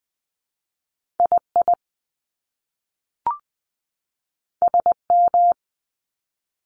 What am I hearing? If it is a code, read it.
Morse code, heard as a keyed tone, about 19 wpm, spaced as for 3 wpm, 700 Hz. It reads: I IS M
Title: Top Two Letter Combinations in Words 20wpm